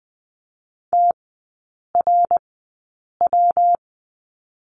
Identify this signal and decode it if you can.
Morse code, heard as a keyed tone, about 20 wpm, spaced as for 8 wpm, 700 Hz. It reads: TRW
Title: 133 Lesson 22e - Question Mark - Three Characters 20wpm